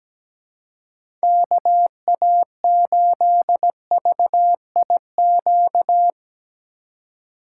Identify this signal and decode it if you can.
Morse code, heard as a keyed tone, about 17 wpm, 700 Hz. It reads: KA8VIQ
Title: USA Call Signs - 6 Characters - 4x Character Spacing - 17wpm